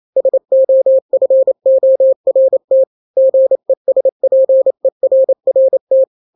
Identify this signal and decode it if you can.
Morse code, heard as a keyed tone, about 28 wpm, 530 Hz. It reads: SOFORT GESPERRT
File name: daily / 27wpm